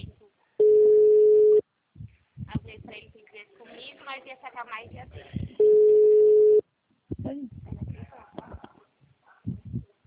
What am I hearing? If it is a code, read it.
Ringback tone (Europe, 425 Hz), with speech.
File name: out- -2008- - - .52408